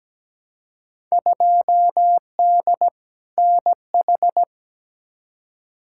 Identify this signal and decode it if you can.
Morse code, heard as a keyed tone, about 17 wpm, 700 Hz. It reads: 2D NH
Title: Field Day Exchanges Encore - 17wpm